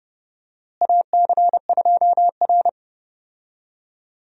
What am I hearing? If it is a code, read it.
Morse code, heard as a keyed tone, about 30 wpm, 700 Hz. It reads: AC2R
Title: USA Call Signs - 4 Characters - Repeated 2x - 30wpm